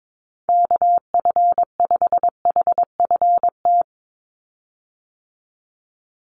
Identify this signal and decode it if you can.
Morse code, heard as a keyed tone, about 22 wpm, 700 Hz. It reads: KF5HFT
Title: USA Call Signs 22wpm